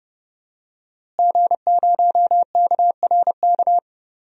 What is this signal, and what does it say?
Morse code, heard as a keyed tone, about 32 wpm, 700 Hz. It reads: G0KRK
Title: DXCC-100 Call Signs - 5 Characters - Repeated 3x - 30wpm